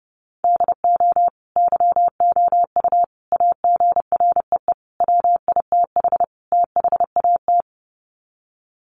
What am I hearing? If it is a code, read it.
Morse code, heard as a keyed tone, about 30 wpm, 700 Hz. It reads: DO YOU AGREE WITH THAT
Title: Sets of 5 Words Encore 30wpm